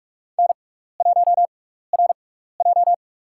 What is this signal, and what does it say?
Morse code, heard as a keyed tone, about 46 wpm, spaced as for 15 wpm, 700 Hz. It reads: N1RJ
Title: USA Call Signs - 4 Characters - 6x Character Spacing - 45wpm